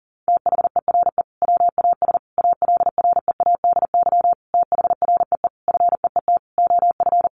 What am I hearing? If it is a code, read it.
Morse code, heard as a keyed tone, about 40 wpm, 700 Hz. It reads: THERE WAS ALREADY THREE FEET OF
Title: Sentences from Top 700 Words 40wpm